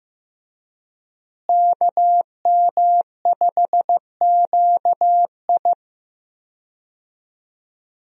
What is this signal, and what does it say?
Morse code, heard as a keyed tone, about 15 wpm, 700 Hz. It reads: KM5QI
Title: USA Call Signs - 5 Characters - Repeated 3x - 15wpm